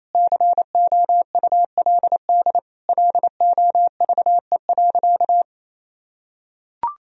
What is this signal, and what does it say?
Morse code, heard as a keyed tone, about 28 wpm, 700 Hz. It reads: COULD LOVE.
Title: Sentences from Top 1000 Words - Review Part 2 - 28wpm